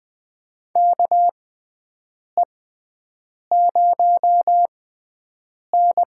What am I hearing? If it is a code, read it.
Morse code, heard as a keyed tone, about 20 wpm, spaced as for 7 wpm, 700 Hz. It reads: KE0N
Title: USA Call Signs - 6 Characters - 6x Character Spacing - 20wpm